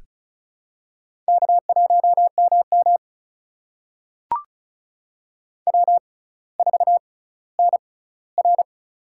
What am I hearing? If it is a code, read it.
Morse code, heard as a keyed tone, about 36 wpm, spaced as for 18 wpm, 700 Hz. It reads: K1MM W4NR
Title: USA Call Signs - 4 Characters - 6x Character Spacing - 35wpm